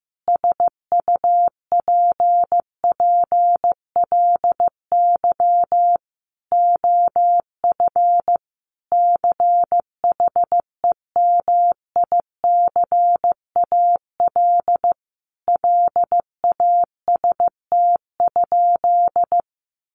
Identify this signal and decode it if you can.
Morse code, heard as a keyed tone, about 15 wpm, 700 Hz. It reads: SUPPLY OF CHEMICAL LAST?
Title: Sentences from Top 1200 Words 15wpm